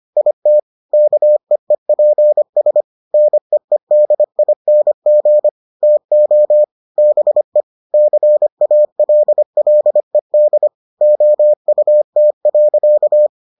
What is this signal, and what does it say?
Morse code, heard as a keyed tone, about 25 wpm, 600 Hz. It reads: IT KEEPS NEEDING TO BE CALLED OUT.